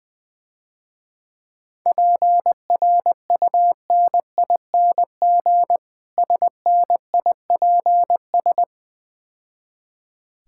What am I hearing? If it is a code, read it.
Morse code, heard as a keyed tone, about 20 wpm, 700 Hz. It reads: PRUNING SNIPS
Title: Difficult Words - Speed-Racing with Hints - 20wpm